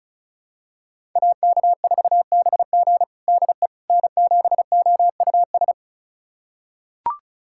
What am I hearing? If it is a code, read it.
Morse code, heard as a keyed tone, about 36 wpm, 700 Hz. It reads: AK4BG DE N7OUS
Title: USA Callsign DE Callsign - 5 Characters - 35wpm